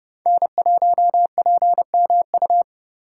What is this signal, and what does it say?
Morse code, heard as a keyed tone, about 32 wpm, 700 Hz. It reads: N1PMU
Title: USA Call Signs - 5 Characters - 3x Character Spacing - 30wpm